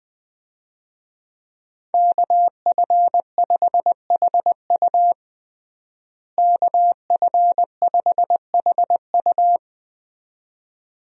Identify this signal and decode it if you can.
Morse code, heard as a keyed tone, about 21 wpm, 700 Hz. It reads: KF5HU KF5HU
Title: USA Call Signs - 5 Characters - Repeated 2x - 20wpm